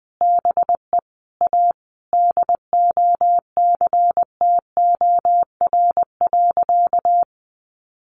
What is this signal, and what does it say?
Morse code, heard as a keyed tone, about 20 wpm, 700 Hz. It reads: BE A DOCTOR.
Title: Easy Phrases 20wpm